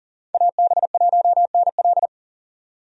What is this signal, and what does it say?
Morse code, heard as a keyed tone, about 40 wpm, 700 Hz. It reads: AB1NL
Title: USA Call Signs - 5 Characters - 40wpm